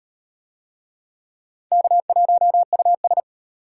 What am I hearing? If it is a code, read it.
Morse code, heard as a keyed tone, about 33 wpm, 700 Hz. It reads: K1US